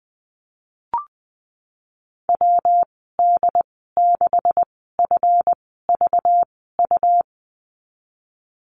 Morse code, heard as a keyed tone, about 20 wpm, spaced as for 15 wpm, 700 Hz. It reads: WD6FVU
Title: 193 Lesson 31g - Letter v - Call Signs 20wpm